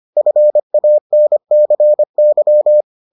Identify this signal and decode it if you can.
Morse code, heard as a keyed tone, about 25 wpm, 600 Hz. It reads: FANCY